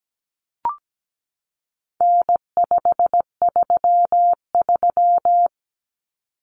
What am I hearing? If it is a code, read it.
Morse code, heard as a keyed tone, about 17 wpm, 700 Hz. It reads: N533
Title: Common Cut Numbers - 4-Digit 17wpm